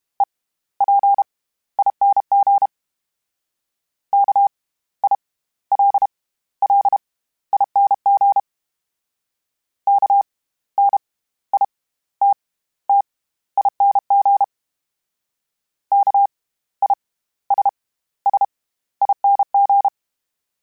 Morse code, heard as a keyed tone, about 32 wpm, spaced as for 16 wpm, 800 Hz. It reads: EPING KILLING KNITTING KISSING
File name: Special INGs / Special INGs-mp3s / SlowToFaster ing @30